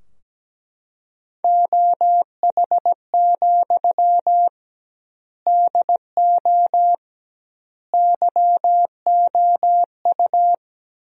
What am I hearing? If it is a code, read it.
Morse code, heard as a keyed tone, about 17 wpm, 700 Hz. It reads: OH, DO YOU
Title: 2x Word-Spacing - Sentences from Top 500 Words - 17wpm